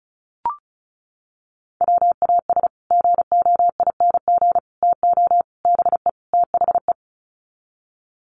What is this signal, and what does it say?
Morse code, heard as a keyed tone, about 34 wpm, 700 Hz. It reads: WAS GOING TO BE THE
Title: Sets of 5 Words Encore2 35wpm